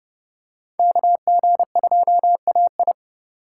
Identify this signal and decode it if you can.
Morse code, heard as a keyed tone, about 30 wpm, 700 Hz. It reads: KG2AI
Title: USA Call Signs - 5 Characters - 4x Character Spacing - 30wpm